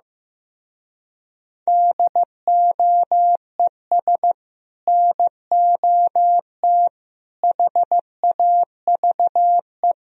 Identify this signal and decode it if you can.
Morse code, heard as a keyed tone, about 15 wpm, 700 Hz. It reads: DOES NOT HAVE
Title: Speed-Racing - Sets of 3 Words 15wpm